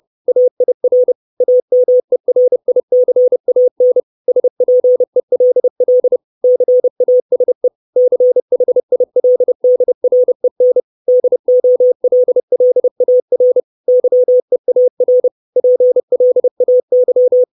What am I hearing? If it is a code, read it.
Morse code, heard as a keyed tone, about 30 wpm, 500 Hz. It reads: AIR AMERICAN SPELL CASE CHILDREN DOLLAR YEAR PLAY
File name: 632 Words 30WPM